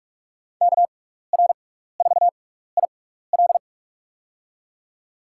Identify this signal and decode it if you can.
Morse code, heard as a keyed tone, about 48 wpm, spaced as for 16 wpm, 700 Hz. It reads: KR4IL
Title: USA Call Signs - 5 Characters - 6x Character Spacing - 45wpm